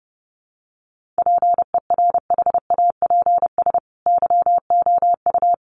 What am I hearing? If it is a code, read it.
Morse code, heard as a keyed tone, about 30 wpm, 700 Hz. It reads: PERHAPS YOU
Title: Sentences from Top 1400 Words 30wpm